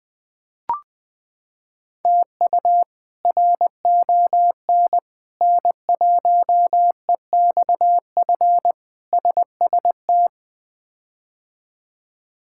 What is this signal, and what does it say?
Morse code, heard as a keyed tone, about 20 wpm, 700 Hz. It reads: TU RON N1EXF SST
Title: K1USN SST QSO Practice - 20wpm